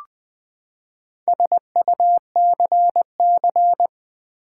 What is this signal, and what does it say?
Morse code, heard as a keyed tone, about 20 wpm, 700 Hz. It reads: SUCC
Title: Build A Word - 4 Letters At A Time - 12-Letter Words - 20wpm